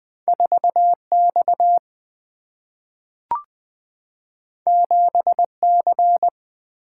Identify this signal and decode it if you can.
Morse code, heard as a keyed tone, about 20 wpm, 700 Hz. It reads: 4 X 7 C
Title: Single-Letter Single-Number 20wpm